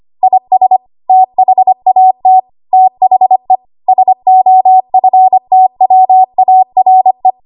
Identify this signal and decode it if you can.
Morse code, heard as a keyed tone, about 25 wpm, 750 Hz. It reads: IS THAT THE SOFTWARE
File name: Archive / 25 WPM / 25WPM